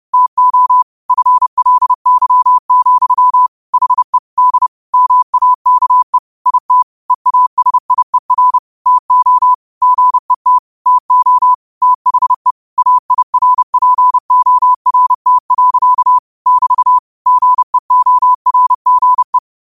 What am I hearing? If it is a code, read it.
Morse code, heard as a keyed tone, about 30 wpm, 1000 Hz. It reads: TO FLY, HE D MAKE IT EASIER TO GET TO THE AIRPORT. <BT> GEORGE